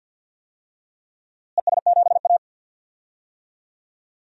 Morse code, heard as a keyed tone, about 46 wpm, 700 Hz. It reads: ES7A
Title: DXCC-100 Call Signs - 4 Characters - Repeated 3x - 50wpm